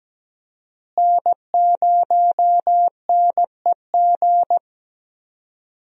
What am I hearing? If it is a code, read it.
Morse code, heard as a keyed tone, about 17 wpm, 700 Hz. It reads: N0NEG
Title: USA Call Signs - 5 Characters - Repeated 3x - 17wpm